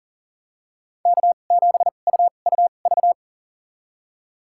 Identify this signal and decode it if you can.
Morse code, heard as a keyed tone, about 40 wpm, spaced as for 29 wpm, 700 Hz. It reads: K7UUV
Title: USA Call Signs - 5 Characters - 2x Character Spacing - 40wpm